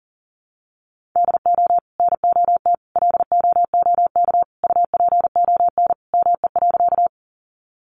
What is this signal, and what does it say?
Morse code, heard as a keyed tone, about 40 wpm, 700 Hz. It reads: DO NOT LOOK UPON ME.
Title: Sentences from Top 1100 Words 40wpm